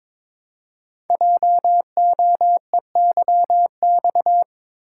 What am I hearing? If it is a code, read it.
Morse code, heard as a keyed tone, about 22 wpm, 700 Hz. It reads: JOEYX
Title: Code Groups - 5 Letter:Numbers - 22wpm